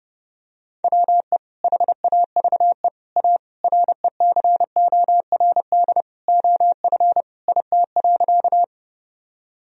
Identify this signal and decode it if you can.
Morse code, heard as a keyed tone, about 30 wpm, 700 Hz. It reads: WE HAVE A RECORD OF IT.
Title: Sentences from Top 1000 Words - Review Part 3 - 30wpm